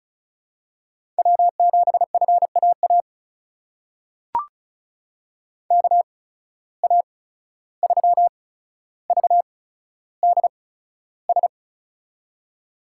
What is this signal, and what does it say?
Morse code, heard as a keyed tone, about 36 wpm, spaced as for 14 wpm, 700 Hz. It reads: W7FAA KA3VDS
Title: USA Call Signs - 8x Character Spacing - 35wpm